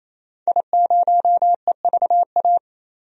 Morse code, heard as a keyed tone, about 28 wpm, 700 Hz. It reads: I0EVA
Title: DXCC-100 Call Signs - 5 Characters - 28wpm